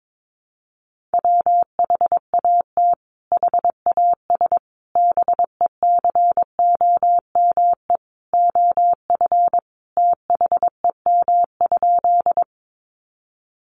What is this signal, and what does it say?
Morse code, heard as a keyed tone, about 22 wpm, 700 Hz. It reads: WHAT HAS BECOME OF THEM?
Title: Sentences from Top 600 Words 22wpm